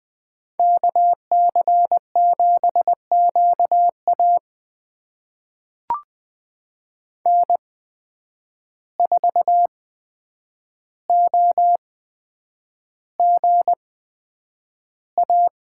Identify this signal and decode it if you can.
Morse code, heard as a keyed tone, about 20 wpm, spaced as for 9 wpm, 700 Hz. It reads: KC7QA N4OGA
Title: USA Call Signs - 5 Characters - 8x Character Spacing - 20wpm